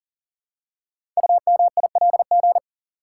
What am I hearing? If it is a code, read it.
Morse code, heard as a keyed tone, about 43 wpm, 700 Hz. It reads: UMILG
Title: Code Groups - 5 Letters - 40wpm